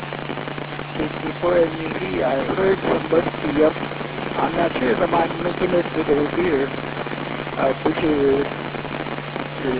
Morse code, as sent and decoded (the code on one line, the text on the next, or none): none